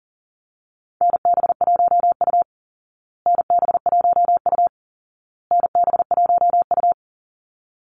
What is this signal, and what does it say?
Morse code, heard as a keyed tone, about 40 wpm, 700 Hz. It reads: NB1U NB1U NB1U
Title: Call Signs Repeated 2x - 40wpm